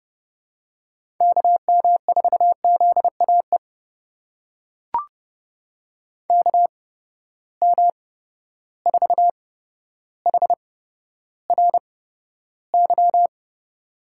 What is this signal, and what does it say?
Morse code, heard as a keyed tone, about 30 wpm, spaced as for 13 wpm, 700 Hz. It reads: KM4ZAE KM4HRY
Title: USA Call Signs - 6 Characters - 8x Character Spacing - 30wpm